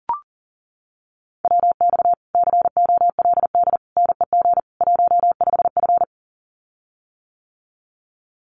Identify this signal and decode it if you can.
Morse code, heard as a keyed tone, about 40 wpm, 700 Hz. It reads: WX COLD NEG 15F
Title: Common QSO Phrases Encore 40wpm